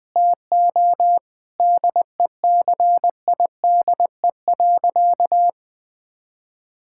Morse code, heard as a keyed tone, about 20 wpm, 700 Hz. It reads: TO DECIDE.
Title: Sentences with 4-Word Phrases - Spoken Beginning - 20wpm